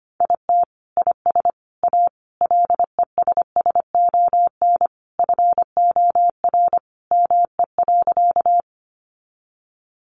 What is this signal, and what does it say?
Morse code, heard as a keyed tone, about 25 wpm, 700 Hz. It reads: IT IS A LESSON FOR ME.
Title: Sentences from Top 1200 Words 25wpm